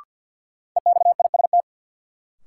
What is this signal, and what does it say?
Morse code, heard as a keyed tone, about 51 wpm, 700 Hz. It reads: EXIST
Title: Sentences from Top 1000 Words - Review Part 5 - New Word Drill - 50wpm